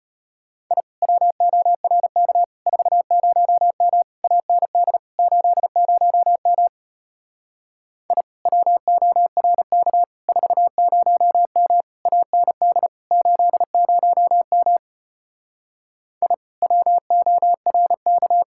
Morse code, heard as a keyed tone, about 33 wpm, 700 Hz. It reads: I WORK 40M AND 80M I WORK 40M AND 80M I WORK